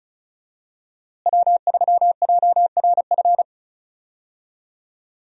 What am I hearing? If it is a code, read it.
Morse code, heard as a keyed tone, about 34 wpm, 700 Hz. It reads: W3JRF
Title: USA Call Signs - 5 Characters - Repeated 2x - 35wpm